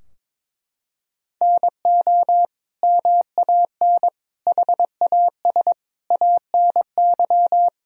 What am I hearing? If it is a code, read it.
Morse code, heard as a keyed tone, about 22 wpm, 700 Hz. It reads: NO MAN HAS ANY
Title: Sentences from Top 1000 Words - Review Part 2 - 22wpm